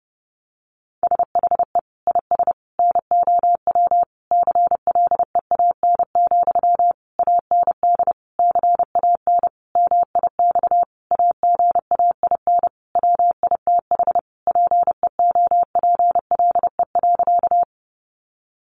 Morse code, heard as a keyed tone, about 30 wpm, 700 Hz. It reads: SHE IS NOW CLEAN, AND CAN MIX AGAIN WITH PEOPLE.
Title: Sentences from Top 1100 Words 30wpm